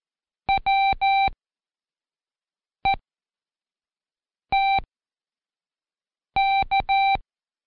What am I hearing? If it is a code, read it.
Morse code, heard as a keyed tone, about 14 wpm, spaced as for 5 wpm, 780 Hz. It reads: WETK
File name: Code Quick / AudioDownld1 / Seg18-Day2-Quiz ASYO